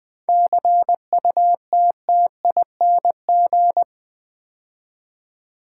Morse code, heard as a keyed tone, about 20 wpm, 700 Hz. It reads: CUTTING
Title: 7 Letter Words - First 3 Letters Given - 20wpm